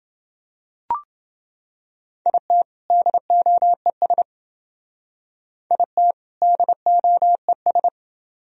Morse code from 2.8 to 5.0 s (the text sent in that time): DOES